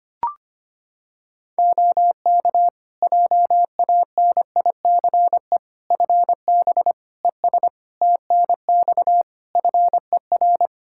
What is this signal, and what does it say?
Morse code, heard as a keyed tone, about 25 wpm, 700 Hz. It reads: OK JANICE FB ES TNX FER
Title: K1USN SST QSO Practice 25wpm